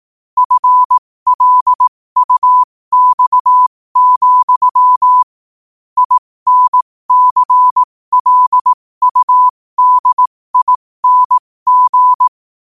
Morse code, heard as a keyed tone, about 18 wpm, spaced as for 15 wpm, 1000 Hz. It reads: FLUX, INCLUDING
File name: qotdfiles / 15wpm